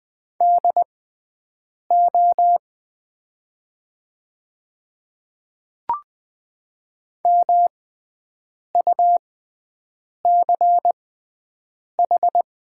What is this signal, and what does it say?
Morse code, heard as a keyed tone, about 20 wpm, spaced as for 7 wpm, 700 Hz. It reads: DO MUCH
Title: 6x Character Spacing - Top 200 Words - 20wpm